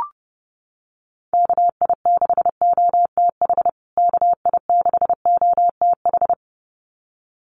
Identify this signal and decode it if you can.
Morse code, heard as a keyed tone, about 30 wpm, 700 Hz. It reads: KI6OTH KI6OTH
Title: USA Call Signs Repeated 2x 30wpm